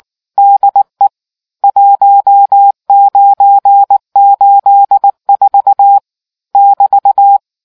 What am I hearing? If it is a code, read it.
Morse code, heard as a keyed tone, about 19 wpm, 800 Hz. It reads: DE 1984 <BT>